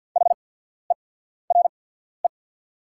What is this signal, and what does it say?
Morse code, heard as a keyed tone, about 49 wpm, spaced as for 14 wpm, 700 Hz. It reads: HERE
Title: 8x Character Spacing - Top 300 Words - 50wpm